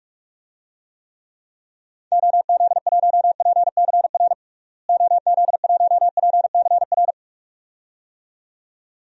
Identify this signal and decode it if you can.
Morse code, heard as a keyed tone, about 44 wpm, 700 Hz. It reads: OZ1PCR OZ1PCR
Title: DXCC-100 Call Signs - 6 Characters - Repeated 2x - 45wpm